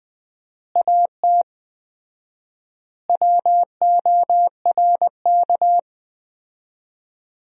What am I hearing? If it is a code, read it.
Morse code, heard as a keyed tone, about 20 wpm, 700 Hz. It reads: AT WORK
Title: 4x Word-Spacing - Sentences from Top 200 Words - 20wpm